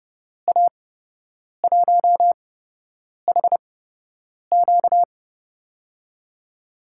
Morse code, heard as a keyed tone, about 31 wpm, spaced as for 8 wpm, 700 Hz. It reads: A1HQ
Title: USA Call Signs - 5 Characters - 8x Character Spacing - 30wpm